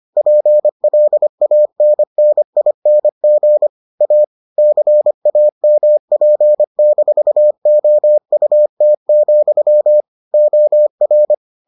Morse code, heard as a keyed tone, about 25 wpm, 600 Hz. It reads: PLANNING A CAMP-OUT, OR